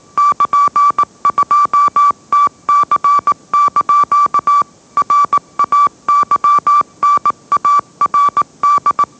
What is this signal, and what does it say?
Morse code, heard as a keyed tone, about 18 wpm, 1160 Hz. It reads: <KN>2TC) RAYNARD
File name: mrx / mrx082 20wpm